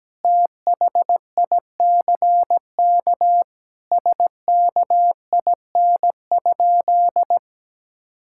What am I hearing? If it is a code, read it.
Morse code, heard as a keyed tone, about 17 wpm, 700 Hz. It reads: THICK SKIN?
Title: Sentences from Top 1600 Words 17wpm